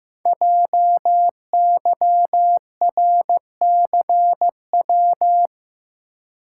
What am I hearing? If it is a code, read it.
Morse code, heard as a keyed tone, about 15 wpm, 700 Hz. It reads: JYRCW
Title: Code Groups - 5 Letter:Numbers - 15wpm